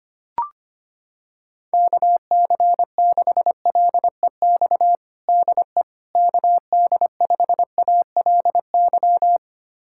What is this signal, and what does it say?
Morse code, heard as a keyed tone, about 25 wpm, 700 Hz. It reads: KC6LEX DE KD5ALY
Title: USA Callsign DE Callsign 25wpm